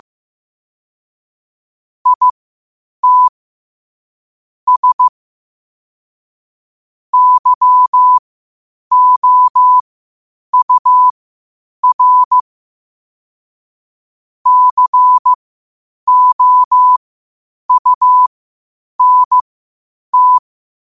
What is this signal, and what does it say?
Morse code, heard as a keyed tone, about 15 wpm, spaced as for 8 wpm, 1000 Hz. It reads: IT S YOUR COUNT
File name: qotdfiles / 8wpm